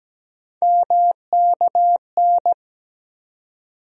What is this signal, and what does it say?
Morse code, heard as a keyed tone, about 17 wpm, 700 Hz. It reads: MKN
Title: USA Call Sign Postfixes - 3 Letters 17wpm